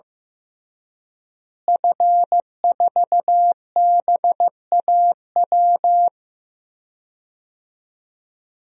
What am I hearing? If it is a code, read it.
Morse code, heard as a keyed tone, about 15 wpm, 700 Hz. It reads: F4BAW
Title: DXCC-100 Call Signs - 5 Characters - Repeated 3x - 15wpm